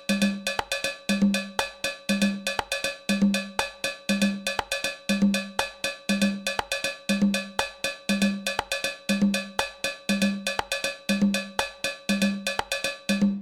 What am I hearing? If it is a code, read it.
There is no code.